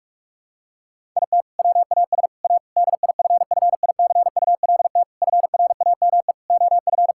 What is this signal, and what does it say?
Morse code, heard as a keyed tone, about 41 wpm, 700 Hz. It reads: IT WAS A DIFFICULT FRAME OF
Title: Sentences from Top 1000 Words - Review Part 6 - 45wpm